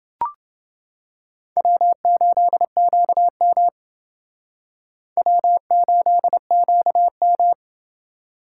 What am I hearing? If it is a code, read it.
Morse code, heard as a keyed tone, about 29 wpm, 700 Hz. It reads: W8QM W8QM